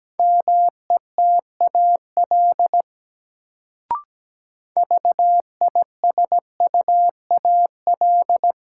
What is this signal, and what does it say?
Morse code, heard as a keyed tone, about 17 wpm, 700 Hz. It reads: METAL VISUAL
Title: Sentences from Top 1000 Words - Review Part 6 - New Word Drill - 17wpm